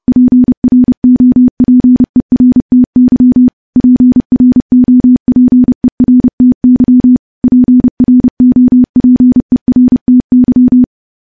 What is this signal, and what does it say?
Morse code, heard as a keyed tone, about 28 wpm, 250 Hz. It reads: PROPERTY PROPERTY PROPERTY
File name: morse / morse-training-30WPM